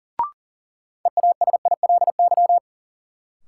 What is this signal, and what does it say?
Morse code, heard as a keyed tone, about 40 wpm, 700 Hz. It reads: EASILY
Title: Sentences from Top 900 Words - New Word Drill - 40wpm